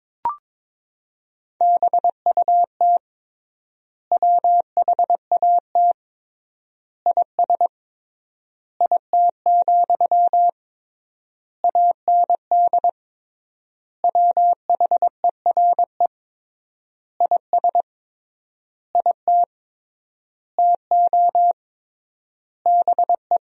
Morse code, heard as a keyed tone, about 22 wpm, 700 Hz. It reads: BUT WHAT IS IT, AND WHERE IS IT TO BE
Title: 3x Word-Spacing - Sentences from Top 300 Words - 22wpm